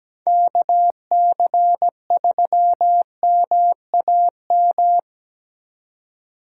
Morse code, heard as a keyed tone, about 20 wpm, 700 Hz. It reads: KC3MAM
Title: Speed Racing - USA 2x3 Call Signs 17wpm